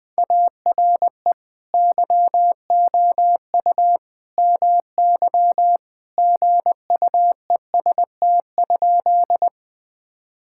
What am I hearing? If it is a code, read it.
Morse code, heard as a keyed tone, about 19 wpm, 700 Hz. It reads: ARE YOU MY GUEST?
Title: Sentences from Top 1600 Words 20wpm